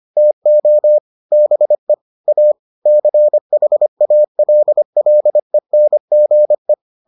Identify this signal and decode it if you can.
Morse code, heard as a keyed tone, about 25 wpm, 600 Hz. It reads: TO BE A CHALLENGE